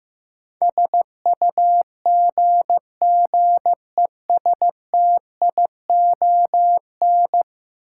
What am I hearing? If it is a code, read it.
Morse code, heard as a keyed tone, about 15 wpm, 700 Hz. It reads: SUGGESTION
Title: Sentences from Top 1600 Words - New Word Drill - 15wpm